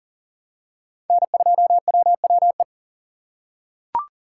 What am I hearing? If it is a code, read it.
Morse code, heard as a keyed tone, about 39 wpm, 700 Hz. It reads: N2WWE
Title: USA Call Signs - 5 Characters - 4x Character Spacing - 40wpm